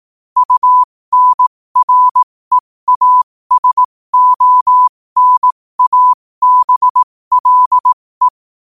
Morse code, heard as a keyed tone, about 18 wpm, spaced as for 15 wpm, 1000 Hz. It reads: UNREASONABLE
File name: qotdfiles / 15wpm